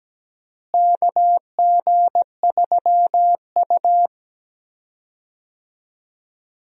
Morse code, heard as a keyed tone, about 17 wpm, 700 Hz. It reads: KG3U
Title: USA Call Signs - 4 Characters - Repeated 2x - 17wpm